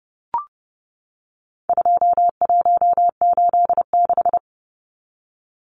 Morse code, heard as a keyed tone, about 28 wpm, 700 Hz. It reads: 2186
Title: Numbers - 4-Digit 30wpm